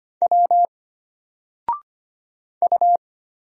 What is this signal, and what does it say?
Morse code, heard as a keyed tone, about 25 wpm, 700 Hz. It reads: W U
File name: Single-Letter-Number-25wpm